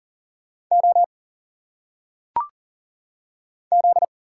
Morse code, heard as a keyed tone, about 40 wpm, 700 Hz. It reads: O Z
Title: Single Letters - Phonetic Pronunciation 40wpm